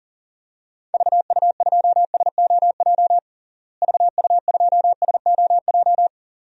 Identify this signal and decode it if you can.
Morse code, heard as a keyed tone, about 40 wpm, 700 Hz. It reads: VU2SOJ VU2SOJ
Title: DXCC-100 Call Signs - 6 Characters - Repeated 2x - 40wpm